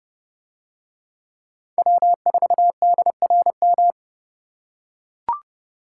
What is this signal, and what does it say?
Morse code, heard as a keyed tone, about 30 wpm, 700 Hz. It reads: W4DRM